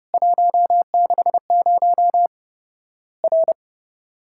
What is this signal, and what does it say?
Morse code, heard as a keyed tone, about 32 wpm, 700 Hz. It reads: 160 R
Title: Worked All Europe DX Contest - QTC Exchange - 30wpm